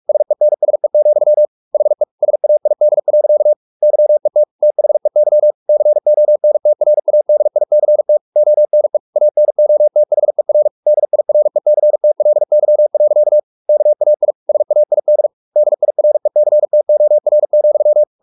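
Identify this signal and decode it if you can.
Morse code, heard as a keyed tone, about 44 wpm, 600 Hz. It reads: SENSE, HE SAID. YET THEY CONTRADICT ONE ANOTHER DIRECTLY. KAI SAID DIRECTOR,
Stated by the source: Morse practice at 45 wpm, the novel Never by Ken Follett in English, from CW med Gnister.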